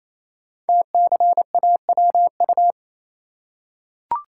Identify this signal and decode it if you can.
Morse code, heard as a keyed tone, about 28 wpm, 700 Hz. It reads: TCAWU
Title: Code Groups - 5 Letter:Numbers - 28wpm